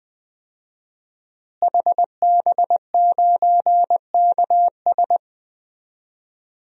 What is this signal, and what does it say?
Morse code, heard as a keyed tone, about 20 wpm, 700 Hz. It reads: HB9KS
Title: DXCC-100 Call Signs - 5 Characters - Repeated 2x - 20wpm